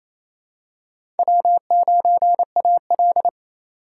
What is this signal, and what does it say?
Morse code, heard as a keyed tone, about 27 wpm, 700 Hz. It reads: W9AL
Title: USA Call Signs - 3x Character Spacing - 28wpm